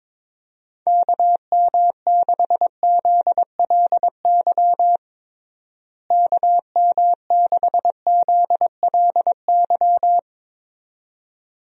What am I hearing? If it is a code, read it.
Morse code, heard as a keyed tone, about 22 wpm, 700 Hz. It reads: KM6ZLY KM6ZLY
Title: USA Call Signs - 6 Characters - Repeated 2x - 22wpm